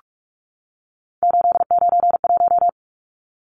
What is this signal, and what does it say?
Morse code, heard as a keyed tone, about 44 wpm, 700 Hz. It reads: 891
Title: Numbers - 3-Digit 45wpm